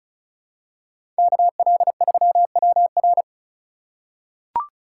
Morse code, heard as a keyed tone, about 36 wpm, 700 Hz. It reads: KL3WR
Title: USA Call Signs - 5 Characters - 3x Character Spacing - 35wpm